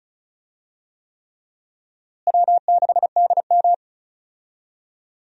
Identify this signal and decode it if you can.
Morse code, heard as a keyed tone, about 34 wpm, 700 Hz. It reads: W6DM
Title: USA Call Signs - 4 Characters - Repeated 3x - 35wpm